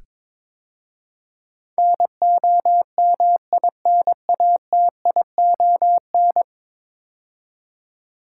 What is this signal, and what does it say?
Morse code, heard as a keyed tone, about 22 wpm, 700 Hz. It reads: NOMINATION
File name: practice / 22wpm / 10 Letter Words - First 3 Letters Given - 22wpm